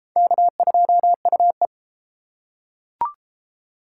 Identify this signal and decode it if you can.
Morse code, heard as a keyed tone, about 32 wpm, 700 Hz. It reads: K2UE